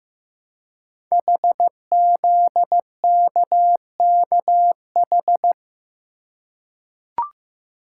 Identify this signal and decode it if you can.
Morse code, heard as a keyed tone, about 15 wpm, 700 Hz. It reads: HZKKH
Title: Code Groups - 5 Letters - 15wpm